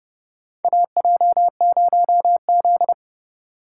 Morse code, heard as a keyed tone, about 30 wpm, 700 Hz. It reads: AJ0Z
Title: USA Call Signs - 8x Character Spacing - 30wpm